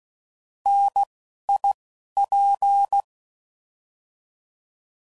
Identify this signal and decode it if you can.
Morse code, heard as a keyed tone, about 16 wpm, 800 Hz. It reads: N I P
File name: Mixed / Contrasts / 3LtrContrasts